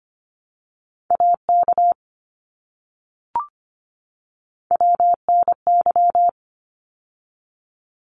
Morse code, heard as a keyed tone, about 25 wpm, 700 Hz. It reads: AK WNY
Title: ARRL Sections 25wpm